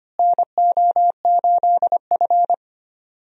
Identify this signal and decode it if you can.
Morse code, heard as a keyed tone, about 25 wpm, 700 Hz. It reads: NO8F
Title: USA Call Signs - 4 Characters - Repeated 3x - 25wpm